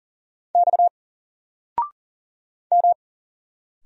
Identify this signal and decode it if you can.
Morse code, heard as a keyed tone, about 39 wpm, 700 Hz. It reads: X M
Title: Single Letter-Number 40wpm